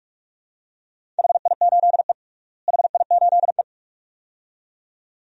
Morse code, heard as a keyed tone, about 41 wpm, 700 Hz. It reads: HI8E HI8E
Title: DXCC-100 Call Signs - 4 Characters - Repeated 2x - 45wpm